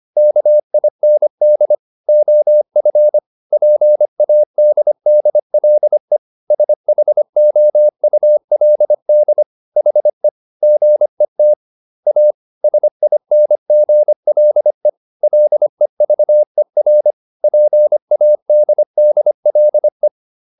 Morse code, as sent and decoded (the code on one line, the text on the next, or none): -.- .. -. -.. / --- ..-. / .--. .- -.. -.. .-.. / ... .... --- ..- .-.. -.. / .... / --. - / .- / ... .. -. --. .-.. / .-.. ...- . .-. / .--. .- -.. -.. .-.. .
KIND OF PADDLE SHOULD HE GET A SINGLE LEVER PADDLE